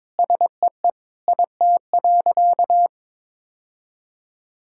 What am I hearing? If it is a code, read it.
Morse code, heard as a keyed tone, about 22 wpm, 700 Hz. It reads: SEE IT.
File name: practice / 22wpm / Sentences with 2-Word Phrases - 22wpm